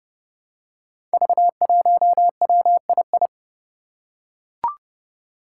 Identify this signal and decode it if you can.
Morse code, heard as a keyed tone, about 29 wpm, 700 Hz. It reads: V1WII
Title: Five Challenging Letter-Number Combinations 30wpm